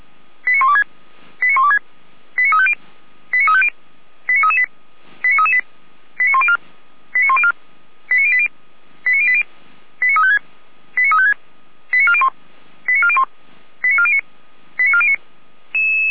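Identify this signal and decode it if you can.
Five-tone selective call (ZVEI), heard as quick runs of five tones, tones of about 80 ms. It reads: 89126 89126 89350 89350 89338 89338 89104 89104 89080 89080 89356 89356 89401 89401 89409 89409
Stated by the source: Five-tone ZVEI selective call received by SDR, from audio.f85360.feuerwehr-orlamuende.org, the Orlamünde fire brigade.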